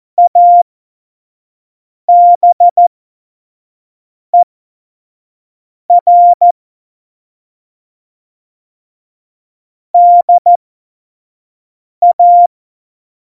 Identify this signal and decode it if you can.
Morse code, heard as a keyed tone, about 14 wpm, spaced as for 5 wpm, 700 Hz. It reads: ABER DA
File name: Morsekurs / Witze / Witz 26. Februar 2019, 25 F70